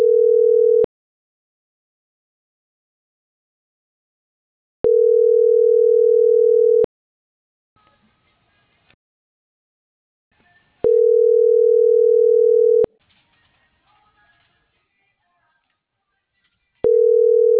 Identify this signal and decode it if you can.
Ringback tone (North America), 440 + 480 Hz.